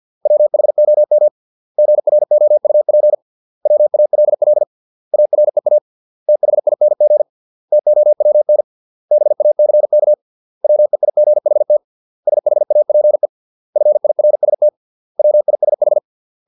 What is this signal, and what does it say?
Morse code, heard as a keyed tone, about 50 wpm, 600 Hz. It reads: WHOM GROUP WALL AREA THING TOWN BACK WEIGHT SHAPE FIRST WISH